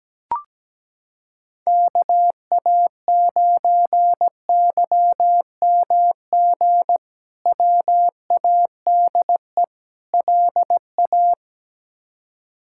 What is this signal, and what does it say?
Morse code, heard as a keyed tone, about 17 wpm, 700 Hz. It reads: KA9YMG WADE LA
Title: K1USN SST QSO Practice - 17wpm